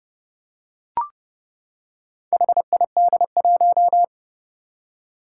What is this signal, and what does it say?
Morse code, heard as a keyed tone, about 30 wpm, 700 Hz. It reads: HID1
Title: Four Challenging Letter-Number Combinations 30wpm